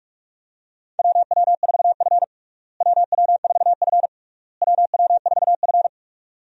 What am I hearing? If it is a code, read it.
Morse code, heard as a keyed tone, about 44 wpm, 700 Hz. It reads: WW4F WW4F WW4F
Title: USA Call Signs - 4 Characters - Repeated 3x - 45wpm